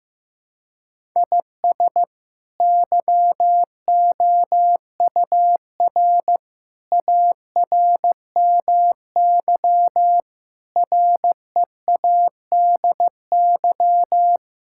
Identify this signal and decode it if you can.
Morse code, heard as a keyed tone, about 15 wpm, 700 Hz. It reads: IS YOUR ARMY READY
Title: Sentences from Top 1000 Words - Review Part 5 - 15wpm